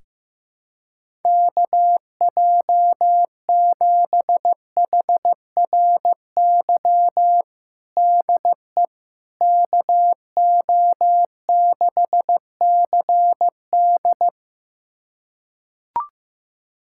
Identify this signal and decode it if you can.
Morse code, heard as a keyed tone, about 15 wpm, 700 Hz. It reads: KJ7HRY DE KO6CD
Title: USA Callsign DE Callsign 15wpm